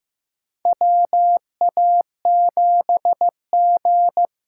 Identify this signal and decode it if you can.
Morse code, heard as a keyed tone, about 15 wpm, 700 Hz. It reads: WA7G
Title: USA Call Signs - 6 Characters - 8x Character Spacing - 15wpm